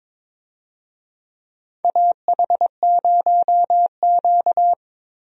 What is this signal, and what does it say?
Morse code, heard as a keyed tone, about 22 wpm, 700 Hz. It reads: AH0Q
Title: USA Call Signs - 4 Characters - 22wpm